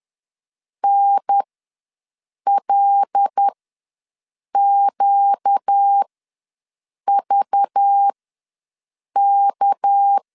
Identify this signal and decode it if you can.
Morse code, heard as a keyed tone, about 10 wpm, spaced as for 6 wpm, 800 Hz. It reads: NLQVK
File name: LEZIONI / 6 PKX0 / 6 PKX ws 10wpm 3706